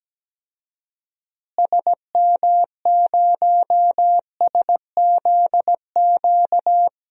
Morse code, heard as a keyed tone, about 17 wpm, 700 Hz. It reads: SM0SZQ
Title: DXCC-100 Call Signs - 6 Characters - Repeated 3x - 17wpm